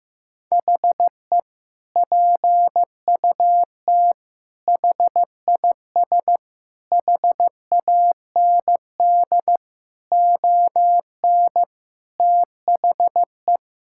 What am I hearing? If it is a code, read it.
Morse code, heard as a keyed tone, about 15 wpm, 700 Hz. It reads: HE PUT HIS HAND ON THE
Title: Sentences from Top 1000 Words - Review Part 6 - 15wpm